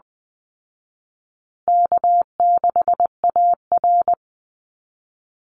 Morse code, heard as a keyed tone, about 20 wpm, 700 Hz. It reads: K6AR
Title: Easy Call Signs Repeated 3x - 20wpm